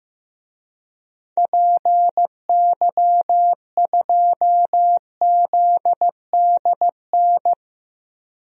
Morse code, heard as a keyed tone, about 16 wpm, 700 Hz. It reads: PY2ZDN
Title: DXCC-100 Call Signs - 6 Characters - Repeated 3x - 15wpm